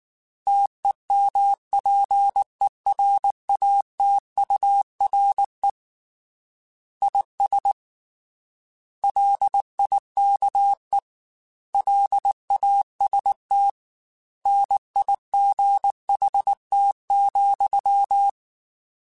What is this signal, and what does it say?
Morse code, heard as a keyed tone, about 19 wpm, 800 Hz. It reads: TEMPERATURE IS LIKE LAST NIGHT,
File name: Weather Sets / Weather Sets-mp3s / Weather 4 17 @18